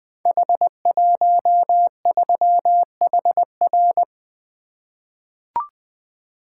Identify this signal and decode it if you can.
Morse code, heard as a keyed tone, about 20 wpm, 700 Hz. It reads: H13HR
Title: Code Groups - 5 Letter:Numbers - 20wpm